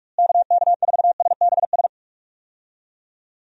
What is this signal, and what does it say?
Morse code, heard as a keyed tone, about 46 wpm, 700 Hz. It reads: KK4SBS
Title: USA Call Signs - 6 Characters - Repeated 3x - 45wpm